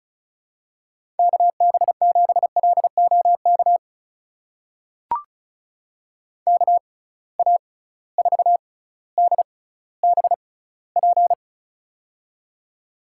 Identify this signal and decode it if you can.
Morse code, heard as a keyed tone, about 35 wpm, spaced as for 18 wpm, 700 Hz. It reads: KB7LOK KA4DBP
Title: USA Call Signs - 6 Characters - 6x Character Spacing - 35wpm